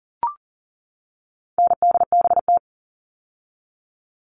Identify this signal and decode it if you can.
Morse code, heard as a keyed tone, about 40 wpm, 700 Hz. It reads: NDBT
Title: All Cut Numbers - 4-Digit 40wpm